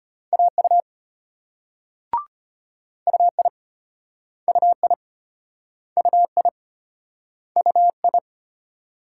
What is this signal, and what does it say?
Morse code, heard as a keyed tone, about 33 wpm, 700 Hz. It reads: AU UI UI UI UI